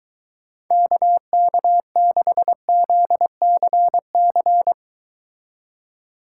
Morse code, heard as a keyed tone, about 22 wpm, 700 Hz. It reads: KK6ZCC